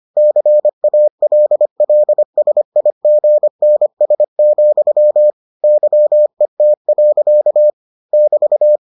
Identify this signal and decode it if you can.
Morse code, heard as a keyed tone, about 25 wpm, 600 Hz. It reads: CALLSIGNS, YET. <BT>